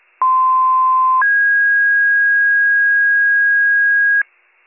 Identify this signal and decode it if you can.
Two-tone page, heard as a one-second tone, then a three-second tone, about 1040 then 1650 Hz. It is heard from 0.2 to 4.2 s.